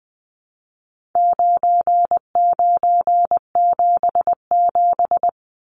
Morse code, heard as a keyed tone, about 19 wpm, 700 Hz. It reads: 9977
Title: Numbers - 4-Digit 20wpm